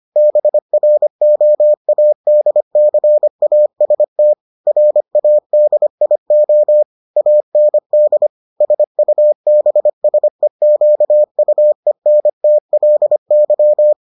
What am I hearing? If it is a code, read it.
Morse code, heard as a keyed tone, about 25 wpm, 600 Hz. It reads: BROADCAST RADIO AND SUBSEQUENTLY